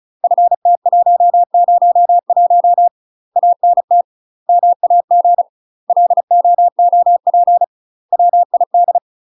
Morse code, heard as a keyed tone, about 34 wpm, 700 Hz. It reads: FT101 ANT MAG LOOP WID
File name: UK MASTER QSOs TRAINING / 35WPM / UK MASTER QSOs 0042